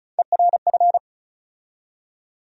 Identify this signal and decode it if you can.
Morse code, heard as a keyed tone, about 34 wpm, 700 Hz. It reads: ERF
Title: USA Call Sign Postfixes - 3 Letters 35wpm